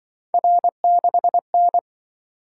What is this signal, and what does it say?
Morse code, heard as a keyed tone, about 24 wpm, 700 Hz. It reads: R6N
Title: Speed-Racing - Three Letter-Number Combinations - 20wpm